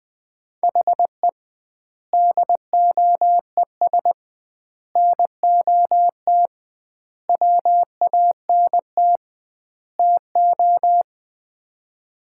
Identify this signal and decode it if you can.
Morse code, heard as a keyed tone, about 20 wpm, 700 Hz. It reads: HE DOES NOT WANT TO
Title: 2x Word-Spacing - Sets of 5 Words - 20wpm